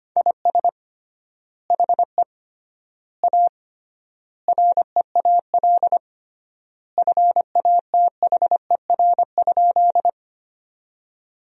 Morse code, heard as a keyed tone, about 25 wpm, 700 Hz. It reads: IS HE A REAL FATHER?
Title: 3x Word-Spacing - Sentences from Top 300 Words - 25wpm